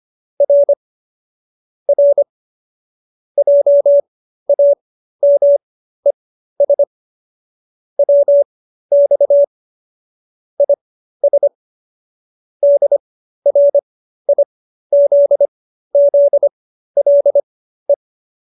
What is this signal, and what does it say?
Morse code, heard as a keyed tone, about 25 wpm, spaced as for 13 wpm, 570 Hz. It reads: R R JAMES WX IS DRIZZLE